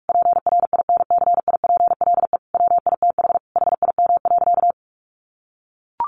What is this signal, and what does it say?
Morse code, heard as a keyed tone, about 44 wpm, 700 Hz. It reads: PRINCIPLE WITH HIM.
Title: Sentences from Top 1400 Words 45wpm